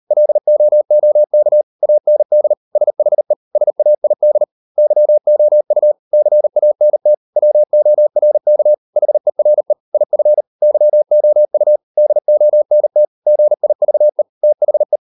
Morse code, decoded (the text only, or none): LOOK AND SHE SAID YOU CANT WORK HERE IF YOU DONT GIVE THE